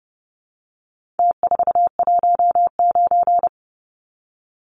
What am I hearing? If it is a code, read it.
Morse code, heard as a keyed tone, about 30 wpm, 700 Hz. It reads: T419
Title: Numbers - Contest Style - 4-Digit 30wpm